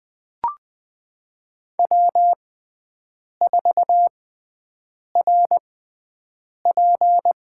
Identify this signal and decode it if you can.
Morse code, heard as a keyed tone, about 20 wpm, spaced as for 7 wpm, 700 Hz. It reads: W4RP
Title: USA Call Signs - 5 Characters - 6x Character Spacing - 20wpm